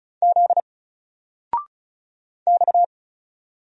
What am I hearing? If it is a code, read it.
Morse code, heard as a keyed tone, about 36 wpm, 700 Hz. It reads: Z X
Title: Single Character (Letter:Number:Punctuation:Prosign) 35wpm